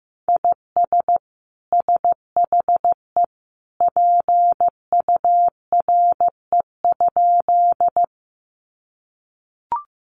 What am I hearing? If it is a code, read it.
Morse code, heard as a keyed tone, about 15 wpm, 700 Hz. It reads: IS SHE PURE?
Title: Sentences from Top 1500 Words 15wpm